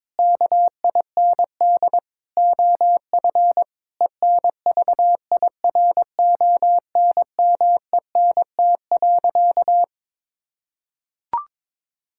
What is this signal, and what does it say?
Morse code, heard as a keyed tone, about 22 wpm, 700 Hz. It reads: KIND OF ENVIRONMENT.
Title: Sentences from Top 1000 Words - Review Part 5 - 22wpm